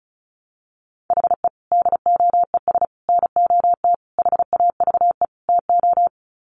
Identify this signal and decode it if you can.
Morse code, heard as a keyed tone, about 34 wpm, 700 Hz. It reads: HE DOES NOT HAVE TO
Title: Sets of 5 Words Encore 35wpm